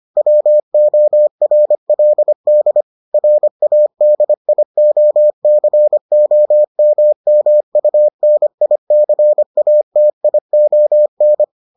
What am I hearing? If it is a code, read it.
Morse code, heard as a keyed tone, about 25 wpm, 600 Hz. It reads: WORLD RADIOCOMMUNICATION